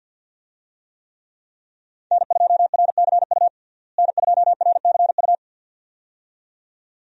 Morse code, heard as a keyed tone, about 52 wpm, 700 Hz. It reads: N2RCU N2RCU
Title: USA Call Signs - 5 Characters - Repeated 2x - 50wpm